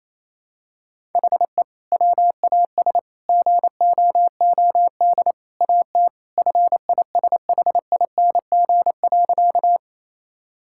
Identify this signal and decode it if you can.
Morse code, heard as a keyed tone, about 28 wpm, 700 Hz. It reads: HE WAS GOOD AT FISHING.
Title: Sentences from Top 1000 Words - Review Part 5 - 28wpm